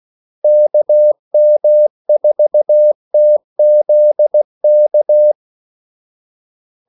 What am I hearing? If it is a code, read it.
Morse code, heard as a keyed tone, about 16 wpm, 600 Hz. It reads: KM4TZK
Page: Kaldesignaler 16 wpm